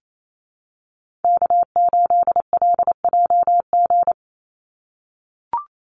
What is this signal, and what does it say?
Morse code, heard as a keyed tone, about 28 wpm, 700 Hz. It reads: K8LJG
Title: NA SOTA Chaser Callsigns 28wpm